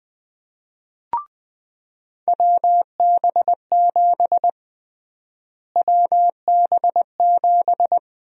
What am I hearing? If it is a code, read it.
Morse code, heard as a keyed tone, about 20 wpm, 700 Hz. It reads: WB7 WB7
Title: USA Call Sign Prefixes - Repeated 2x - 20wpm